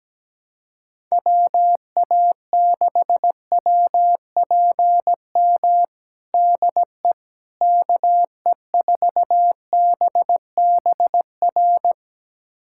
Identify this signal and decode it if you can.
Morse code, heard as a keyed tone, about 17 wpm, 700 Hz. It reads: WA6WPM DE KE4BBR
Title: USA Callsign DE Callsign - 6 Characters - 17wpm